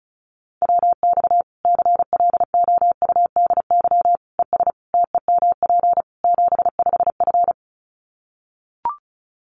Morse code, heard as a keyed tone, about 34 wpm, 700 Hz. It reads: WX CLOUDY ES TEMP 75F
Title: Common QSO Phrases 35wpm